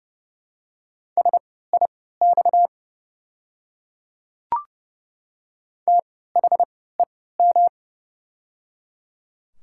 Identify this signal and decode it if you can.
Morse code, heard as a keyed tone, about 30 wpm, spaced as for 17 wpm, 700 Hz. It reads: SIX THEM
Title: 3x Character Spacing - Top 500 Words - 30wpm